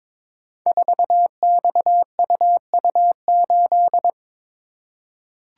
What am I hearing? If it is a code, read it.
Morse code, heard as a keyed tone, about 22 wpm, 700 Hz. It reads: 4XUU8
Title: Code Groups - 5 Letter:Numbers - 22wpm